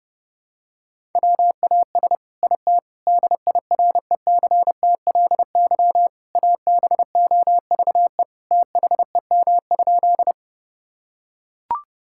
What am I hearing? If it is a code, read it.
Morse code, heard as a keyed tone, about 32 wpm, 700 Hz. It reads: WAS IT DIRECTLY ABOVE THEM?
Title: Sentences from Top 1000 Words - Review Part 5 - 30wpm